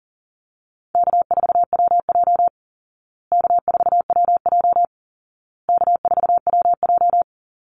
Morse code, heard as a keyed tone, about 40 wpm, 700 Hz. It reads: K4WJ K4WJ K4WJ
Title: Easy Call Signs Repeated 3x - 40wpm